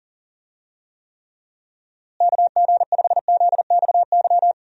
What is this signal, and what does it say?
Morse code, heard as a keyed tone, about 37 wpm, 700 Hz. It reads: KG5ZXY
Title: USA Call Signs - 6 Characters - Repeated 2x - 40wpm